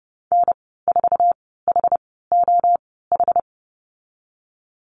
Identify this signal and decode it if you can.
Morse code, heard as a keyed tone, about 30 wpm, spaced as for 17 wpm, 700 Hz. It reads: N4HOH
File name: Lesson 15g - Number 5 - Call Signs 30wpm